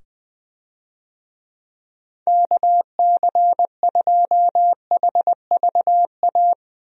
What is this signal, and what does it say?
Morse code, heard as a keyed tone, about 20 wpm, 700 Hz. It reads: KC2HVA
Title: USA Call Signs - 6 Characters - 20wpm